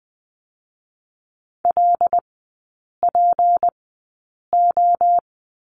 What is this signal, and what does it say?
Morse code, heard as a keyed tone, about 20 wpm, spaced as for 8 wpm, 700 Hz. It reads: LPO
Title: 152 Lesson 25e - Letter p - Three Characters 20wpm